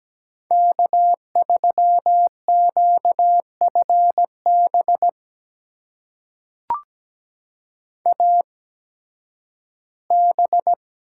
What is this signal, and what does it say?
Morse code, heard as a keyed tone, about 17 wpm, spaced as for 11 wpm, 700 Hz. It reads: K3QFB AB
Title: USA Call Signs - 5 Characters - 8x Character Spacing - 17wpm